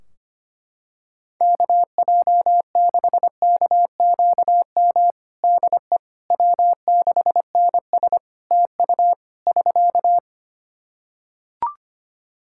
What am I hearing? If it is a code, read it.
Morse code, heard as a keyed tone, about 25 wpm, 700 Hz. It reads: KJ6KQM DE W6NS TU <SK>
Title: Practice QSO - Exchange 4 - Ending - 25wpm